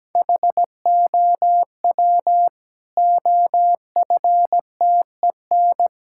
Morse code, heard as a keyed tone, about 17 wpm, 700 Hz. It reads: HOW OFTEN
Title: Sentences from Top 1000 Words - Review Part 5 - 17wpm